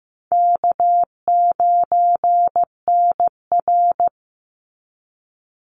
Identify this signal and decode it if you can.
Morse code, heard as a keyed tone, about 15 wpm, 700 Hz. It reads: K9NR
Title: Easy Call Signs - 15wpm